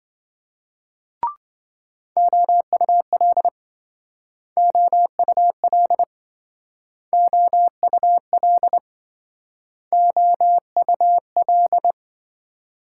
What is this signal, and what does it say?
Morse code, heard as a keyed tone, about 27 wpm, 700 Hz. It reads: OUL OUL OUL OUL